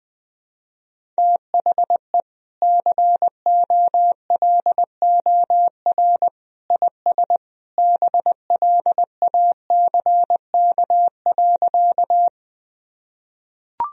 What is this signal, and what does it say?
Morse code, heard as a keyed tone, about 21 wpm, 700 Hz. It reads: THE COLOR IS BLACK.
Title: Sentences from Top 1000 Words - Review Part 2 - 20wpm